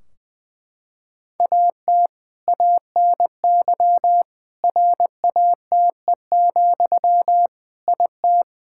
Morse code, heard as a keyed tone, about 20 wpm, 700 Hz. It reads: AT ANY RATE, IT
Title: Sentences from Top 1600 Words 20wpm